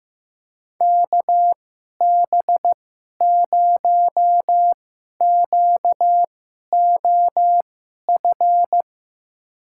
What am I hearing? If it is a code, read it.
Morse code, heard as a keyed tone, about 15 wpm, spaced as for 11 wpm, 700 Hz. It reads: KB0QOF
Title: USA Call Signs - 6 Characters - 2x Character Spacing - 15wpm